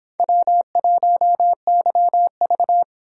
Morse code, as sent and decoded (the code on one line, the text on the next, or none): .-- .---- -.-- ...-
W1YV